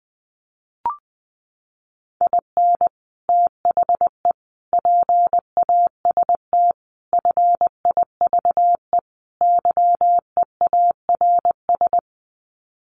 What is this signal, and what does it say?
Morse code, heard as a keyed tone, about 19 wpm, 700 Hz. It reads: IN THE PAST FIVE YEARS
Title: Sets of 5 Words Encore 20wpm